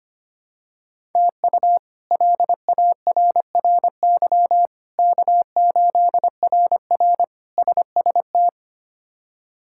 Morse code, heard as a keyed tone, about 25 wpm, 700 Hz. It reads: TU LARRY K8RR SST
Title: K1USN SST QSO Practice - 25wpm